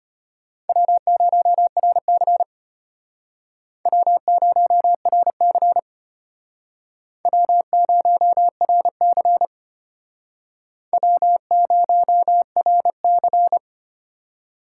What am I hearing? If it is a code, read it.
Morse code, heard as a keyed tone, about 33 wpm, 700 Hz. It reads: W0RC W0RC W0RC W0RC